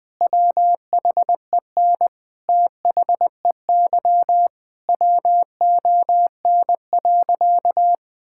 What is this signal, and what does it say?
Morse code, heard as a keyed tone, about 21 wpm, 700 Hz. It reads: WHEN THEY WON.
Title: Sentences with 2-Word Phrases - 20wpm